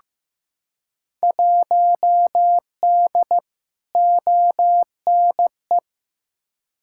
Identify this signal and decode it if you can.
Morse code, heard as a keyed tone, about 15 wpm, 700 Hz. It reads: 1D ONE
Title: Field Day Exchanges - 15wpm